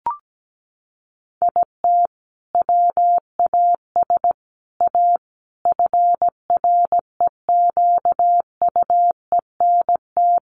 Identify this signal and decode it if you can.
Morse code, heard as a keyed tone, about 17 wpm, 700 Hz. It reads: IT WAS A FREQUENT
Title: Sentences from Top 1400 Words 17wpm